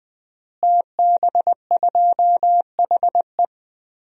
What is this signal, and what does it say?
Morse code, heard as a keyed tone, about 21 wpm, 700 Hz. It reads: TB2HE
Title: DXCC-100 Call Signs - 5 Characters - Repeated 3x - 20wpm